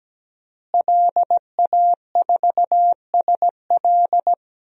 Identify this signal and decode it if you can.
Morse code, heard as a keyed tone, about 17 wpm, 700 Hz. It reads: LA4SL
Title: DXCC-100 Call Signs - 5 Characters - Repeated 3x - 17wpm